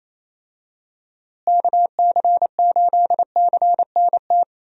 Morse code, heard as a keyed tone, about 28 wpm, 700 Hz. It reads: KC8CNT
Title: USA Call Signs - 6 Characters - Repeated 2x - 28wpm